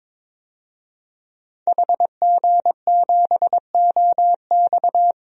Morse code, heard as a keyed tone, about 22 wpm, 700 Hz. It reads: HG7OX
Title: DXCC-100 Call Signs - 5 Characters - Repeated 2x - 22wpm